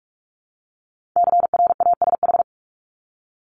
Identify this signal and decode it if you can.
Morse code, heard as a keyed tone, about 46 wpm, 700 Hz. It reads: CRASH
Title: Multi-letter Phonemes 45wpm